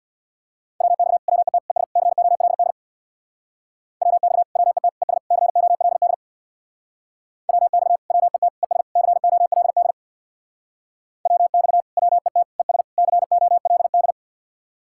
Morse code, heard as a keyed tone, about 64 wpm, 700 Hz. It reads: WX WET ES COLD WX WET ES COLD WX WET ES COLD WX WET ES COLD